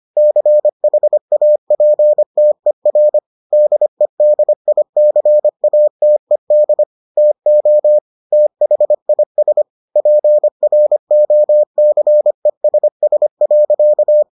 Morse code, heard as a keyed tone, about 25 wpm, 600 Hz. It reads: CHAPTER DEDICATED TO THIS PROCESS.